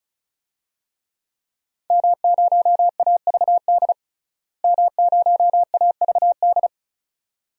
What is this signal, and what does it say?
Morse code, heard as a keyed tone, about 36 wpm, 700 Hz. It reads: M0AVD M0AVD
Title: DXCC-100 Call Signs - 5 Characters - Repeated 2x - 35wpm